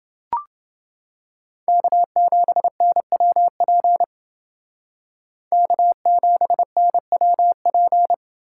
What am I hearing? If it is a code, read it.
Morse code, heard as a keyed tone, about 28 wpm, 700 Hz. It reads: K7NWP K7NWP